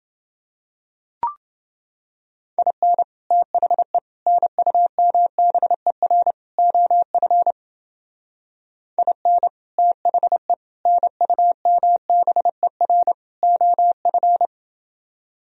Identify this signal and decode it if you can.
Morse code, heard as a keyed tone, about 28 wpm, 700 Hz. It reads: IN THE NUMBER OF IN THE NUMBER OF